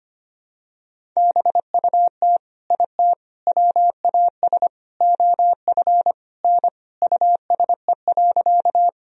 Morse code, heard as a keyed tone, about 25 wpm, 700 Hz. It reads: BUT IT WAS OF N USE.
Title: Sentences from Top 100 Words and Abbreviations 25wpm